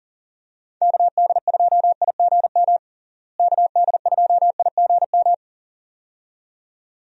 Morse code, heard as a keyed tone, about 37 wpm, 700 Hz. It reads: KD2IGM KD2IGM
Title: USA Call Signs - 6 Characters - Repeated 2x - 40wpm